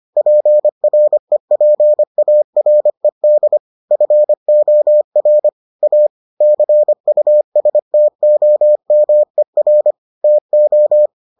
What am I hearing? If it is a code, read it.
Morse code, heard as a keyed tone, about 25 wpm, 600 Hz. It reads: PREPARED FOR A CUSTOMER TO